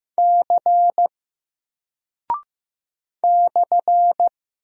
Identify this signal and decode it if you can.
Morse code, heard as a keyed tone, about 15 wpm, 700 Hz. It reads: C /
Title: Single Character (Letter:Number:Punctuation:Prosign) 15wpm